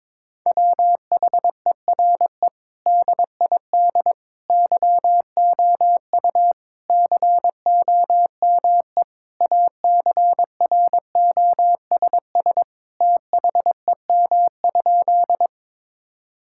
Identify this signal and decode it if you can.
Morse code, heard as a keyed tone, about 22 wpm, 700 Hz. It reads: WHERE DID YOU COME ACROSS THEM?
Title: Sentences from Top 1000 Words - Review Part 5 - 22wpm